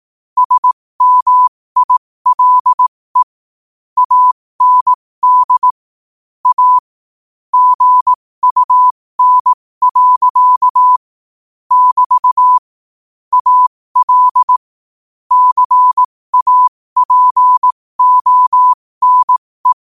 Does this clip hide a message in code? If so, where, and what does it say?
Yes: Morse code from 0.4 to 19.7 s, heard as a keyed tone, about 18 wpm, spaced as for 15 wpm, 1000 Hz. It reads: SMILE AND A GUN. <BT> AL CAPONE